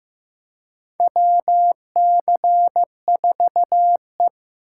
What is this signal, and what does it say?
Morse code, heard as a keyed tone, about 16 wpm, 700 Hz. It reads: WC4E
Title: USA Call Signs - 4 Characters - 4x Character Spacing - 15wpm